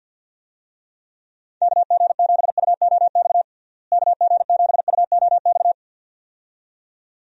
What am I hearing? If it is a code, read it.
Morse code, heard as a keyed tone, about 50 wpm, 700 Hz. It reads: KG7UOX KG7UOX
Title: USA Call Signs - 6 Characters - Repeated 2x - 50wpm